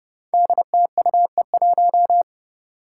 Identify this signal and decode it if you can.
Morse code, heard as a keyed tone, about 30 wpm, 700 Hz. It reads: DTUE1
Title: Five Challenging Letter-Number Combinations 30wpm